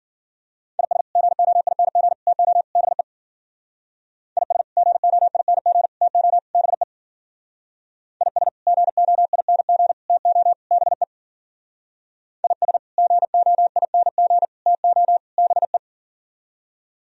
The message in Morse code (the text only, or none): IS GOING TO BE IS GOING TO BE IS GOING TO BE IS GOING TO BE